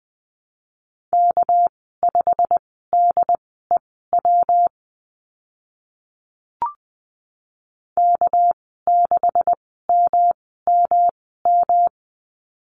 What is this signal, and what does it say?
Morse code, heard as a keyed tone, about 20 wpm, spaced as for 14 wpm, 700 Hz. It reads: K5DEW K6MMM
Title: 213 Lesson 34g - Number 8 - Call Signs 20wpm